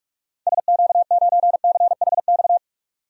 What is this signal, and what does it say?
Morse code, heard as a keyed tone, about 43 wpm, 700 Hz. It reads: SQ9CHX
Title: DXCC-100 Call Signs - 6 Characters - 45wpm